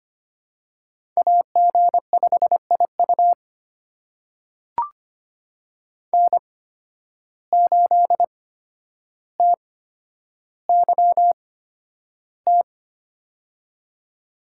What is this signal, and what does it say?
Morse code, heard as a keyed tone, about 25 wpm, spaced as for 11 wpm, 700 Hz. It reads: AG5IU N8TYT
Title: USA Call Signs - 5 Characters - 8x Character Spacing - 25wpm